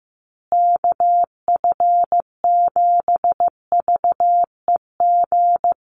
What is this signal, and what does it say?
Morse code, heard as a keyed tone, about 15 wpm, 700 Hz. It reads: KF7VEG
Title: USA Call Signs Repeated 2x 15wpm